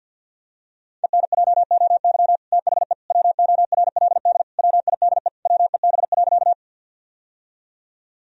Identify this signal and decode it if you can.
Morse code, heard as a keyed tone, about 53 wpm, 700 Hz. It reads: ENJOY THE WORLD WIDE WEB.
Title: Sentences from Top 1000 Words - Review Part 4 - 50wpm